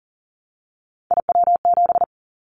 Morse code, heard as a keyed tone, about 40 wpm, 700 Hz. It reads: IW7
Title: Callsign Prefixes for Top 25 DXCC - 40wpm